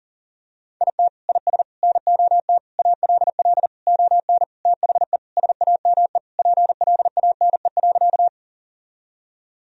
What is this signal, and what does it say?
Morse code, heard as a keyed tone, about 40 wpm, 700 Hz. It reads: IT IS NOT ALL ON THE SAME PLANE.
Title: Sentences from Top 1000 Words - Review Part 3 - 40wpm